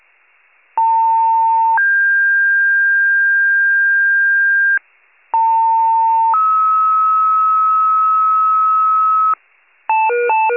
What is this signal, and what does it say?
Two-tone page, heard as a one-second tone, then a three-second tone, about 900 then 1600 Hz, about 900 then 1290 Hz, with noise. It is heard from 0.8 to 9.3 s.
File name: wcalert